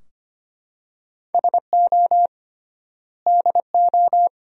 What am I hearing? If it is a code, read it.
Morse code, heard as a keyed tone, about 25 wpm, 700 Hz. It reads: SO DO
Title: 3x Word-Spacing - Sentences from Top 400 Words - 25wpm